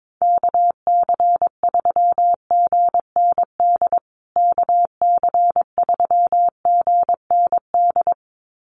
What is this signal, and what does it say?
Morse code, heard as a keyed tone, about 22 wpm, 700 Hz. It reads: KC3GND KC3GND
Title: USA Call Signs Repeated 2x 22wpm